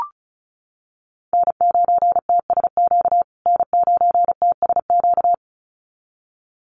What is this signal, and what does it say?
Morse code, heard as a keyed tone, about 34 wpm, 700 Hz. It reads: N9TSQ N9TSQ
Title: USA Call Signs Repeated 2x 35wpm